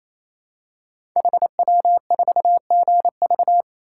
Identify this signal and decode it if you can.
Morse code, heard as a keyed tone, about 27 wpm, 700 Hz. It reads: HW4GV
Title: Code Groups - 5 Letter:Numbers - 28wpm